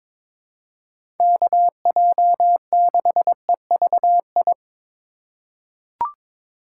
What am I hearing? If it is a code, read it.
Morse code, heard as a keyed tone, about 21 wpm, 700 Hz. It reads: KJ6EVI
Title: USA Call Signs - 6 Characters - 4x Character Spacing - 22wpm